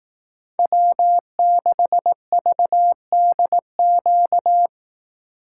Morse code, heard as a keyed tone, about 18 wpm, 700 Hz. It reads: W6VDQ
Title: Speed Racing - USA 1x3 Call Signs 15wpm